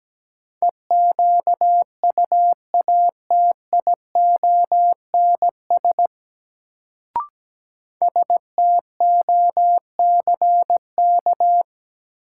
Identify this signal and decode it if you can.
Morse code, heard as a keyed tone, about 17 wpm, 700 Hz. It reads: EQUATIONS STOCK
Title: Sets of 5 Words - New Word Drill - 17wpm